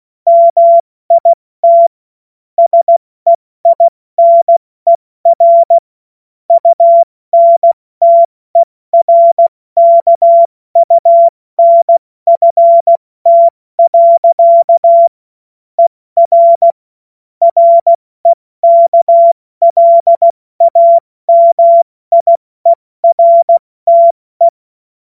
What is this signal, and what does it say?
Morse code, heard as a keyed tone, about 16 wpm, 680 Hz. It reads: MIT SEINER UNTERKUNFT. ER REKLAMIERTE
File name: Morsekurs / Witze / Witz 25. Oktober 2022 70 F80